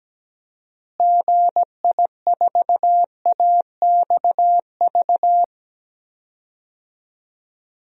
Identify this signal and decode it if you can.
Morse code, heard as a keyed tone, about 17 wpm, 700 Hz. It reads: GI4AXV
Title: DXCC-100 Call Signs - 6 Characters - Repeated 3x - 17wpm